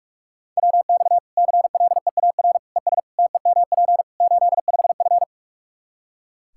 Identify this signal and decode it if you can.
Morse code, heard as a keyed tone, about 45 wpm, 700 Hz. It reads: WX CLEAR ES TEMP 85F
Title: Practice QSO - Exchange 2 - Info - 45wpm